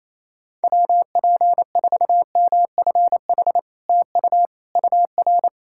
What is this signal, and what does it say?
Morse code, heard as a keyed tone, about 27 wpm, 690 Hz. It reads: WP4MFH TU UR
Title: SOTA QSO Practice 28wpm